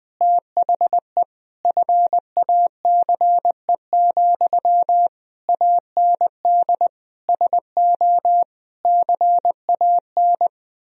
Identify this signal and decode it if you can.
Morse code, heard as a keyed tone, about 20 wpm, 700 Hz. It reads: THE FACE, AND SO CAN
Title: Sentences from Top 1000 Words - Review Part 2 - 20wpm